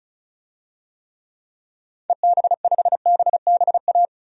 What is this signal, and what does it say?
Morse code, heard as a keyed tone, about 36 wpm, 700 Hz. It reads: EB5BBA
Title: DXCC-100 Call Signs - 6 Characters - Repeated 3x - 35wpm